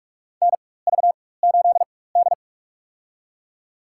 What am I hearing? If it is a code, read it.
Morse code, heard as a keyed tone, about 45 wpm, spaced as for 21 wpm, 700 Hz. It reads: NV8D
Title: USA Call Signs - 4 Characters - 4x Character Spacing - 45wpm